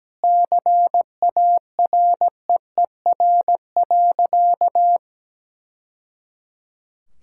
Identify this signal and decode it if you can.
Morse code, heard as a keyed tone, about 17 wpm, 700 Hz. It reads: CAREER.
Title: Sentences from Top 1000 Words - Review Part 6 - 17wpm